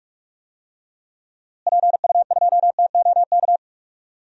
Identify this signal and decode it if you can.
Morse code, heard as a keyed tone, about 46 wpm, 700 Hz. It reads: PU2TOK
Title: DXCC-100 Call Signs - 6 Characters - Repeated 3x - 45wpm